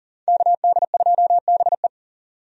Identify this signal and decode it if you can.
Morse code, heard as a keyed tone, about 40 wpm, 700 Hz. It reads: KD2BE
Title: USA Call Signs - 5 Characters - 40wpm